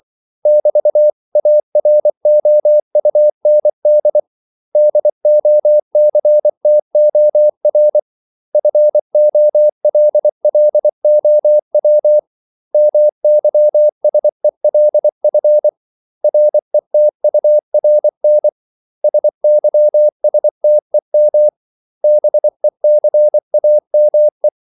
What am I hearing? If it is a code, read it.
Morse code, heard as a keyed tone, about 24 wpm, 600 Hz. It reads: <BT>AROUND DOCTOR FOLLOW MYSELF RETURN SYSTEM BECAME
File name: CW / Practice / 500MostCommonEnglishWords24